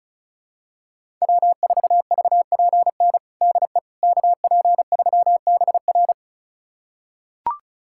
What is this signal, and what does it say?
Morse code, heard as a keyed tone, about 34 wpm, 700 Hz. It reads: W4VPN DE KP3BR
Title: USA Callsign DE Callsign - 5 Characters - 35wpm